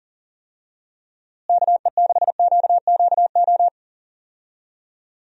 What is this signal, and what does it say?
Morse code, heard as a keyed tone, about 40 wpm, 700 Hz. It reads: KE6QQO
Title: USA Call Signs - 6 Characters - Repeated 3x - 40wpm